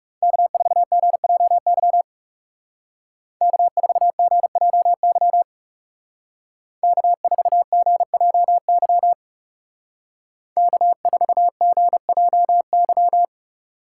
Morse code, heard as a keyed tone, about 37 wpm, 700 Hz. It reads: K4GJY K4GJY K4GJY K4GJY
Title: Speed Racing - USA 1x3 Call Signs 30wpm